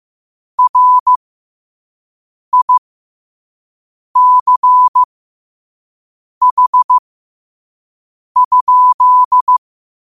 Morse code, heard as a keyed tone, about 15 wpm, spaced as for 5 wpm, 1000 Hz. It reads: RICH?